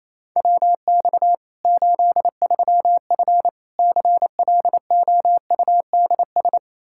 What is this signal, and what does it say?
Morse code, heard as a keyed tone, about 28 wpm, 700 Hz. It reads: WX 83F CLOUDS
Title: Practice QSO - Exchange 2 - Info - 28wpm